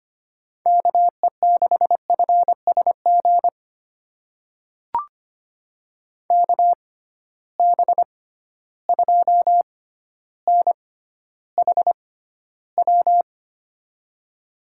Morse code, heard as a keyed tone, about 25 wpm, spaced as for 13 wpm, 700 Hz. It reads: KE6FSG KB2NHW
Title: USA Call Signs - 6 Characters - 6x Character Spacing - 25wpm